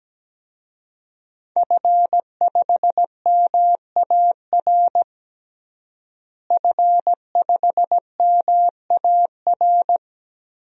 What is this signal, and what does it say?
Morse code, heard as a keyed tone, about 17 wpm, 700 Hz. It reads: F5MAR F5MAR
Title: DXCC-100 Call Signs - 5 Characters - Repeated 2x - 17wpm